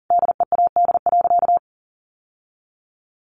Morse code, heard as a keyed tone, about 40 wpm, 700 Hz. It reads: DEAD.
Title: Sentences from Top 1500 Words 40wpm